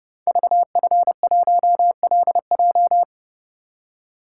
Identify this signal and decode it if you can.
Morse code, heard as a keyed tone, about 28 wpm, 700 Hz. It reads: VF1LJ
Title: Code Groups - 5 Letter:Numbers - 30wpm